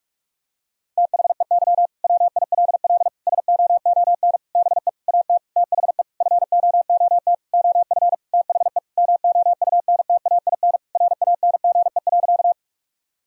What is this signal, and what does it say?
Morse code, heard as a keyed tone, about 46 wpm, 700 Hz. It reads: THEY WILL SOON BE AT THE FOOT OF THE MOUNTAIN RANGE.
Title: Sentences from Top 1000 Words - Review Part 4 - 45wpm